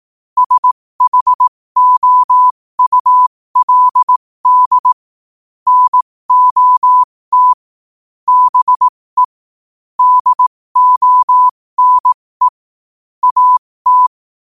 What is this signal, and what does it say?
Morse code, heard as a keyed tone, about 18 wpm, spaced as for 15 wpm, 1000 Hz. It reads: SHOULD NOT BE DONE AT